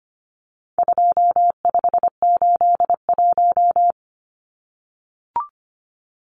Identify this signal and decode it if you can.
Morse code, heard as a keyed tone, about 25 wpm, 700 Hz. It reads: 2581
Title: Numbers - 4-Digit 25wpm